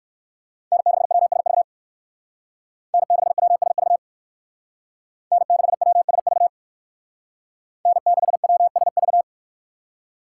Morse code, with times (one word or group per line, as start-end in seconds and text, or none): 0.7-1.6 N6WSV
2.9-4.0 N6WSV
5.3-6.5 N6WSV
7.9-9.2 N6WSV